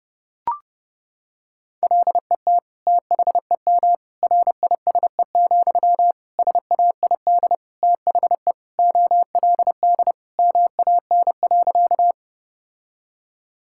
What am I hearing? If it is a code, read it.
Morse code, heard as a keyed tone, about 30 wpm, 700 Hz. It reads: LET THEM RISE, SAID THE OLD MAN.
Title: Sentences from Top 1000 Words - Review Part 6 - 30wpm